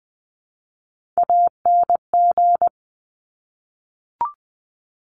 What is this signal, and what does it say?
Morse code, heard as a keyed tone, about 20 wpm, 700 Hz. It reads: ANG
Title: Top Three Letter Combinations in Words 20wpm